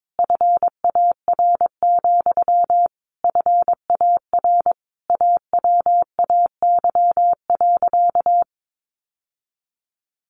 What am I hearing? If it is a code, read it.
Morse code, heard as a keyed tone, about 22 wpm, 700 Hz. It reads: FAR, FAR AWAY.
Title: Sentences from Top 600 Words 22wpm